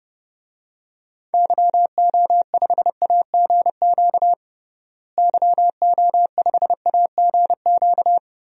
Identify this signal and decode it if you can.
Morse code, heard as a keyed tone, about 30 wpm, 700 Hz. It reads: YO5AGQ YO5AGQ
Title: DXCC-100 Call Signs - 6 Characters - Repeated 2x - 30wpm